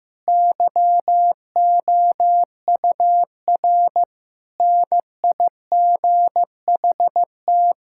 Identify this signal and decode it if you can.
Morse code, heard as a keyed tone, about 15 wpm, 700 Hz. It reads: YOUR NIGHT
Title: Sentences from Top 1600 Words 15wpm